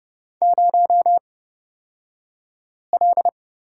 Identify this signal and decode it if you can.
Morse code, heard as a keyed tone, about 30 wpm, spaced as for 5 wpm, 700 Hz. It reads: 0L
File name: Lesson 38d - Letter q - Two Characters 30wpm